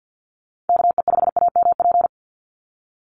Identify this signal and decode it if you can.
Morse code, heard as a keyed tone, about 50 wpm, 700 Hz. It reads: KE5AMP
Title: USA Call Signs 50wpm